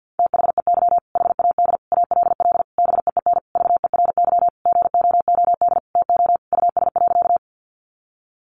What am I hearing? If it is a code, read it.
Morse code, heard as a keyed tone, about 50 wpm, 700 Hz. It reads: THEY HAD ALL BEEN VERY GOOD TO US.
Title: Sentences from Top 200 Words 50wpm